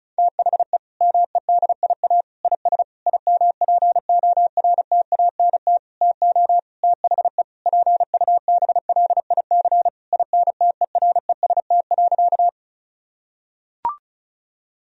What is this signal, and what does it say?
Morse code, heard as a keyed tone, about 36 wpm, 700 Hz. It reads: THE MEDIA IS IMPORTANT TO THE PUBLIC INTEREST.
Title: Sentences from Top 1000 Words - Review Part 4 - 35wpm